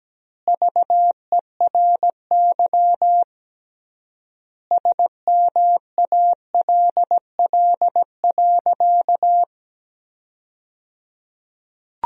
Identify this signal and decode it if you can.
Morse code, heard as a keyed tone, about 17 wpm, 700 Hz. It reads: VERY SMALL.
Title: 3x Word-Spacing - Sentences from Top 400 Words - 17wpm